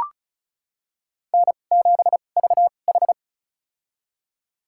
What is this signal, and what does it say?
Morse code, heard as a keyed tone, about 34 wpm, spaced as for 25 wpm, 700 Hz. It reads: N7VH
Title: USA Call Signs - 4 Characters - 2x Character Spacing - 35wpm